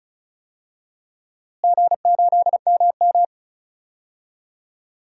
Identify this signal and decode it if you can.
Morse code, heard as a keyed tone, about 34 wpm, 700 Hz. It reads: G8MM
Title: DXCC-100 Call Signs - 4 Characters - Repeated 3x - 35wpm